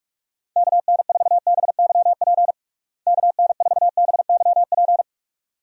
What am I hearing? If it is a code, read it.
Morse code, heard as a keyed tone, about 47 wpm, 700 Hz. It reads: KN4BYP KN4BYP
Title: USA Call Signs - 6 Characters - Repeated 2x - 45wpm